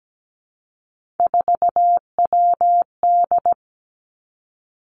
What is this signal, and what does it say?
Morse code, heard as a keyed tone, about 17 wpm, 700 Hz. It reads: 4WD
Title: Three Challenging Letter-Number Combinations 17wpm